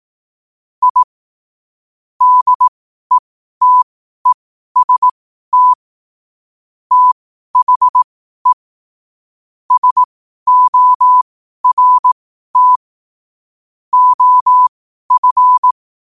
Morse code, heard as a keyed tone, about 18 wpm, spaced as for 12 wpm, 1000 Hz. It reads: I DETEST THE SORT OF
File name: qotdfiles / 12wpm